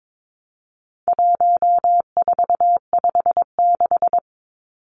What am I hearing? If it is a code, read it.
Morse code, heard as a keyed tone, about 22 wpm, 700 Hz. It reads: 1456
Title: Numbers - 4-Digit 22wpm